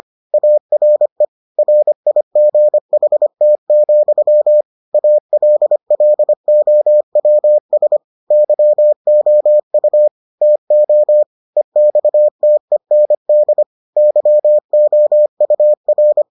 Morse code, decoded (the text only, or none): ARE RIGHT, ALLOWS YOU TO EXTEND YOUR